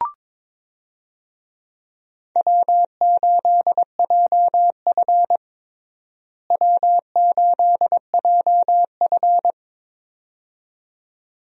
Morse code, heard as a keyed tone, about 21 wpm, 700 Hz. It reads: W8JF W8JF
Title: USA Call Signs - 4 Characters - Repeated 2x - 22wpm